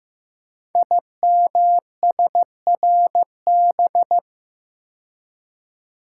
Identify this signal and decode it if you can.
Morse code, heard as a keyed tone, about 15 wpm, 700 Hz. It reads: IMSRB
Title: Code Groups - 5 Letters - 15wpm